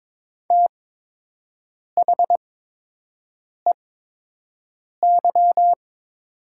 Morse code, heard as a keyed tone, about 22 wpm, spaced as for 6 wpm, 700 Hz. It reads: THEY
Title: 8x Character Spacing - Top 200 Words - 22wpm